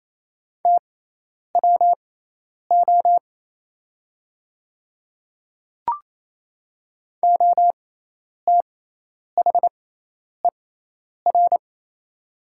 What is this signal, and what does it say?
Morse code, heard as a keyed tone, about 28 wpm, spaced as for 10 wpm, 700 Hz. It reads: TWO OTHER
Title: 6x Character Spacing - Top 100 Words - 28wpm